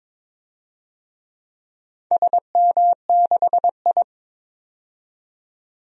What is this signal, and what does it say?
Morse code, heard as a keyed tone, about 22 wpm, 700 Hz. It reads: SM6I
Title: DXCC-100 Call Signs - 4 Characters - 22wpm